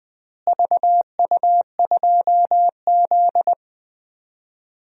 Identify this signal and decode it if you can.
Morse code, heard as a keyed tone, about 20 wpm, 700 Hz. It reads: VU2Z
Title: DXCC-100 Call Signs - 4 Characters - Repeated 3x - 20wpm